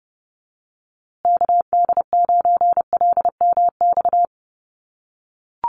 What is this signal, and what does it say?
Morse code, heard as a keyed tone, about 30 wpm, 700 Hz. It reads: KD9LMX
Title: USA Call Signs Repeated 3x 30wpm